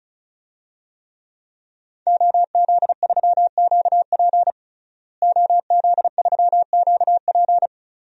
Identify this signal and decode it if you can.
Morse code, heard as a keyed tone, about 36 wpm, 700 Hz. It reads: OZ3QP OZ3QP
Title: DXCC-100 Call Signs - 5 Characters - Repeated 3x - 35wpm